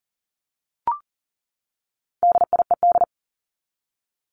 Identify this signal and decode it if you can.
Morse code, heard as a keyed tone, about 40 wpm, 700 Hz. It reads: DIED
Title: ED Suffix - Top 100 Words with Base Words 40wpm